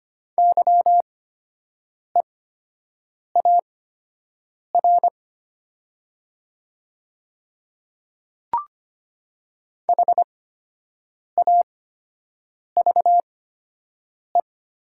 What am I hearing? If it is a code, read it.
Morse code, heard as a keyed tone, about 25 wpm, spaced as for 7 wpm, 700 Hz. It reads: YEAR HAVE
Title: 8x Character Spacing - Top 200 Words - 25wpm